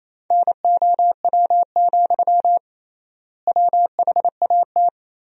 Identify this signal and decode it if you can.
Morse code, heard as a keyed tone, about 28 wpm, 700 Hz. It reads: NOW, WHAT
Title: 3x Word-Spacing - Sentences from Top 200 Words - 28wpm